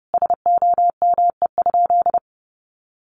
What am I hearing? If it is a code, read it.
Morse code, heard as a keyed tone, about 30 wpm, 700 Hz. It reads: SOME?
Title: Sentences from Top 100 Words 30wpm